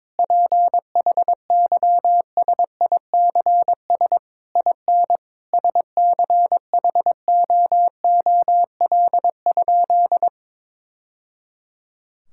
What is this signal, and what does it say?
Morse code, heard as a keyed tone, about 22 wpm, 700 Hz. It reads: PHYSICS IN SCHOOL?
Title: Sentences from Top 1000 Words - Review Part 5 - 22wpm